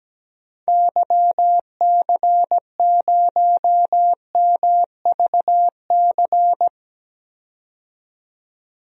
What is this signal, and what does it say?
Morse code, heard as a keyed tone, about 17 wpm, 700 Hz. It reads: YC0MVC
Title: DXCC-100 Call Signs - 6 Characters - 17wpm